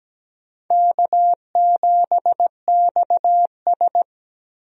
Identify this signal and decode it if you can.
Morse code, heard as a keyed tone, about 17 wpm, 700 Hz. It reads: K7XS
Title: USA Call Signs - 4 Characters - Repeated 2x - 17wpm